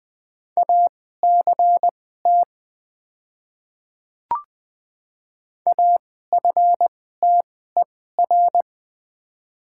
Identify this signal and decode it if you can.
Morse code, heard as a keyed tone, about 20 wpm, spaced as for 14 wpm, 700 Hz. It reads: ACT AFTER
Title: 2x Character Spacing - Top 200 Words - 20wpm